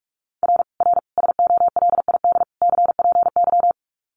Morse code, heard as a keyed tone, about 44 wpm, 700 Hz. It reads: R R SOLID CPY
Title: Common QSO Phrases 45wpm